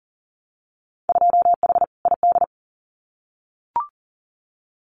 Morse code, heard as a keyed tone, about 40 wpm, 700 Hz. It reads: 2H ID
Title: Winter Field Day Exchanges 40wpm